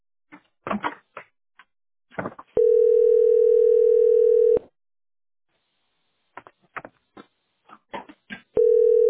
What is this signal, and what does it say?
Ringback tone (North America), 440 + 480 Hz.